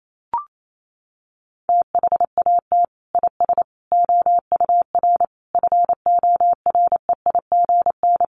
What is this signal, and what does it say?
Morse code, heard as a keyed tone, about 28 wpm, 700 Hz. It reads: THAT IS OUR FOREIGN
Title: Sentences from Top 1400 Words 28wpm